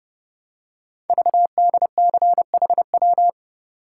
Morse code, heard as a keyed tone, about 30 wpm, 700 Hz. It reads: VDCHW
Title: Code Groups - 5 Letters - 30wpm